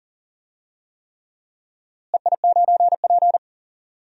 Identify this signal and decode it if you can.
Morse code, heard as a keyed tone, about 38 wpm, 700 Hz. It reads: EI9P
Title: DXCC-100 Call Signs - 4 Characters - Repeated 3x - 40wpm